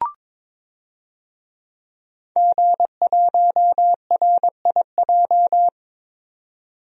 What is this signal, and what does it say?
Morse code, heard as a keyed tone, about 22 wpm, 700 Hz. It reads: G1RIJ
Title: DXCC-100 Call Signs - 5 Characters - 22wpm